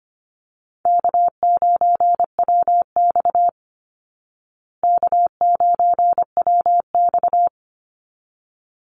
Morse code, heard as a keyed tone, about 25 wpm, 700 Hz. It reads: K9WX K9WX
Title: Call Signs Repeated 3x - 25wpm